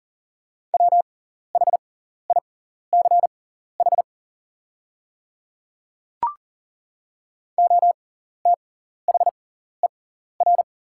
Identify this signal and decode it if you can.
Morse code, heard as a keyed tone, about 40 wpm, spaced as for 14 wpm, 690 Hz. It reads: WHICH OTHER
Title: 6x Character Spacing - Top 100 Words - 40wpm